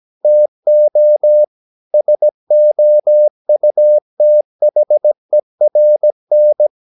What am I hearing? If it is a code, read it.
Morse code, heard as a keyed tone, about 17 wpm, 600 Hz. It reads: TO SOUTHERN
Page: Romanen Never af Ken Follet på engelsk.